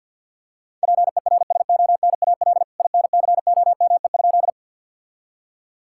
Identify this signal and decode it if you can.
Morse code, heard as a keyed tone, about 49 wpm, 700 Hz. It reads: PERSONAL INCOME?
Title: Sentences from Top 1000 Words - Review Part 5 - 50wpm